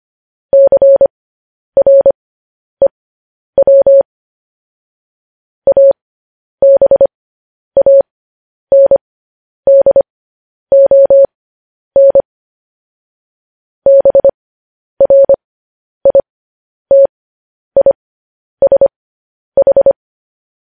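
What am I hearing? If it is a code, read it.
Morse code, heard as a keyed tone, about 25 wpm, spaced as for 10 wpm, 550 Hz. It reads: CREW ABANDON BRITISH
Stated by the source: Morse code news headlines practice transmission.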